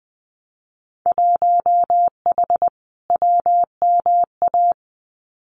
Morse code, heard as a keyed tone, about 20 wpm, 700 Hz. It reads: 1H WMA
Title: Winter Field Day Exchanges 20wpm